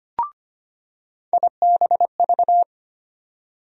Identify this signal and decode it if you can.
Morse code, heard as a keyed tone, about 25 wpm, 700 Hz. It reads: IBV
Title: USA Call Sign Postfixes - 3 Letters 25wpm